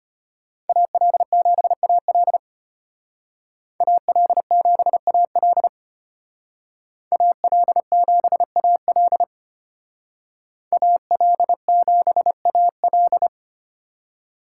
Morse code, heard as a keyed tone, about 32 wpm, 700 Hz. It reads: AL7AL AL7AL AL7AL AL7AL